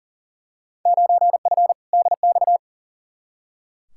Morse code, heard as a keyed tone, about 39 wpm, 700 Hz. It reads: 9F DX
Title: Field Day Exchanges Encore - 40wpm